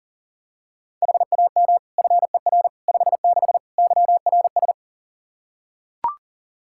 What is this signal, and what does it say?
Morse code, heard as a keyed tone, about 40 wpm, 700 Hz. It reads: HAM FER 56 YRS
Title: Practice QSO - Exchange 3 - More Info - 40wpm